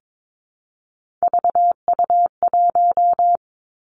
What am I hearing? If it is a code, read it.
Morse code, heard as a keyed tone, about 22 wpm, 700 Hz. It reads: VU1
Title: Three Challenging Letter-Number Combinations 22wpm